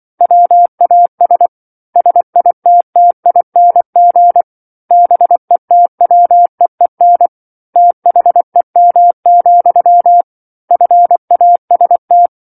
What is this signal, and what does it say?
Morse code, heard as a keyed tone, about 24 wpm, 700 Hz. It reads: WAS SITTING BETWEEN THEM, FAST
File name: 07-Chapter-07 24 WPM